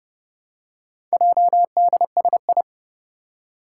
Morse code, heard as a keyed tone, about 30 wpm, 700 Hz. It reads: JDSI
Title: Four Challenging Letter-Number Combinations 30wpm